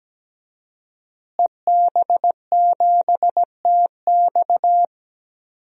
Morse code, heard as a keyed tone, about 17 wpm, 700 Hz. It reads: EB7TX
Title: DXCC-100 Call Signs - 5 Characters - Repeated 2x - 17wpm